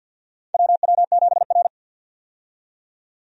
Morse code, heard as a keyed tone, about 51 wpm, 700 Hz. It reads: WW7R
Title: USA Call Signs - 4 Characters - 50wpm